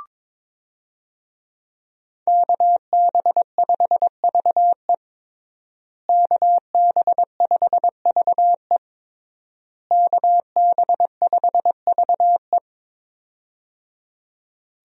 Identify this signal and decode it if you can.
Morse code, heard as a keyed tone, about 22 wpm, 700 Hz. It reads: KB5VE KB5VE KB5VE
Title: USA Call Signs - 5 Characters - Repeated 3x - 22wpm